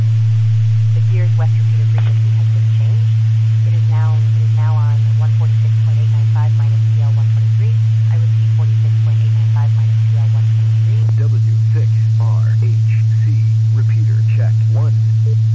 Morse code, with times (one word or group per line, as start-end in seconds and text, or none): none